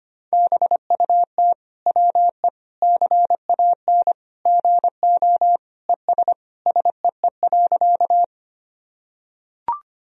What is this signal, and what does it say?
Morse code, heard as a keyed tone, about 25 wpm, 700 Hz. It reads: BUT WE CAN GO ES SEE.
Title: Sentences from Top 100 Words and Abbreviations 25wpm